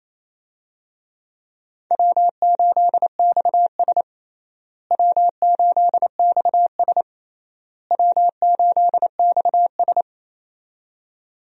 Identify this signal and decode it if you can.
Morse code, heard as a keyed tone, about 28 wpm, 700 Hz. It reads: W8XS W8XS W8XS
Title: USA Call Signs - 4 Characters - Repeated 3x - 28wpm